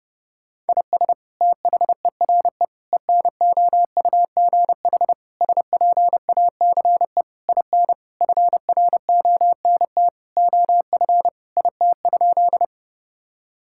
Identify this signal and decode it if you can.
Morse code, heard as a keyed tone, about 30 wpm, 700 Hz. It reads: IS THERE ENOUGH SPACE IN FRONT OF IT?
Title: Sentences from Top 1000 Words - Review Part 5 - 30wpm